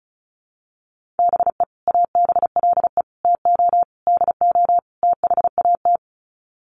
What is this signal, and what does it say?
Morse code, heard as a keyed tone, about 34 wpm, 700 Hz. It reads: BE ABLE TO DO THAT
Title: Sets of 5 Words Encore2 35wpm